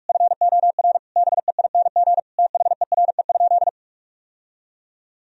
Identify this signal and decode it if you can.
Morse code, heard as a keyed tone, about 41 wpm, 700 Hz. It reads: FOR BEING THERE?
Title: Sentences from Top 1000 Words - Review Part 2 - 45wpm